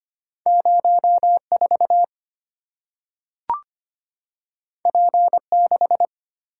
Morse code, heard as a keyed tone, about 25 wpm, 700 Hz. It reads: 04 P6
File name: Two Letter-Number Combinations - 25wpm